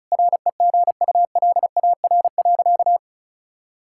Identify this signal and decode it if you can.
Morse code, heard as a keyed tone, about 36 wpm, 700 Hz. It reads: REGULAR.
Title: Sentences from Top 1000 Words - Review Part 6 - 35wpm